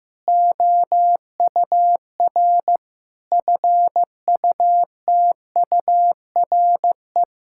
Morse code, heard as a keyed tone, about 16 wpm, 700 Hz. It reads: OUR FUTURE
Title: Sentences from Top 1000 Words - Review Part 6 - 15wpm